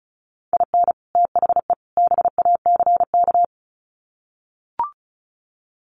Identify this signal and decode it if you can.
Morse code, heard as a keyed tone, about 34 wpm, 700 Hz. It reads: IN THE BACK
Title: Sets of 3 Words 35wpm